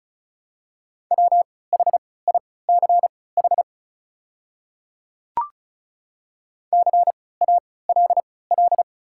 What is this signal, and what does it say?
Morse code, heard as a keyed tone, about 36 wpm, spaced as for 20 wpm, 700 Hz. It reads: WHICH CALL
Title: 3x Character Spacing - Top 200 Words - 35wpm